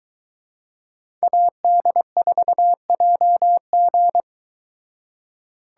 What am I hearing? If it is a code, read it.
Morse code, heard as a keyed tone, about 23 wpm, 700 Hz. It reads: AD4JG